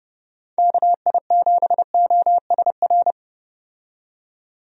Morse code, heard as a keyed tone, about 30 wpm, 700 Hz. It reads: KI7OSR
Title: USA Call Signs - 6 Characters - 30wpm